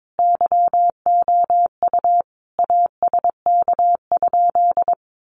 Morse code, heard as a keyed tone, about 22 wpm, 700 Hz. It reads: YOU ASK?
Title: Sentences from Top 200 Words 22wpm